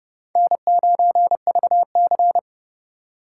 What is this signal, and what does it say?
Morse code, heard as a keyed tone, about 40 wpm, 700 Hz. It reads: N9VC